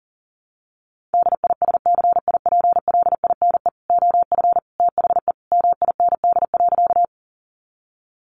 Morse code, heard as a keyed tone, about 40 wpm, 700 Hz. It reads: DISCIPLINE OF THE MIND.
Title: Sentences from Top 900 Words 40wpm